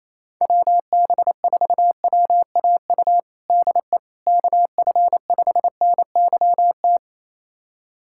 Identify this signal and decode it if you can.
Morse code, heard as a keyed tone, about 27 wpm, 700 Hz. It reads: WB4WAU DE KF5NYT
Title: USA Callsign DE Callsign - 6 Characters - 28wpm